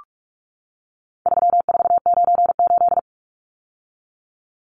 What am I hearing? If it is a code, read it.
Morse code, heard as a keyed tone, about 44 wpm, 700 Hz. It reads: 3498
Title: Numbers - 4-Digit 45wpm